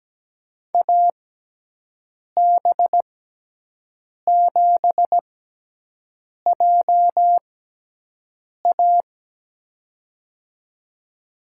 Morse code, heard as a keyed tone, about 17 wpm, spaced as for 6 wpm, 700 Hz. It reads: AB7JA
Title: USA Call Signs - 5 Characters - 6x Character Spacing - 17wpm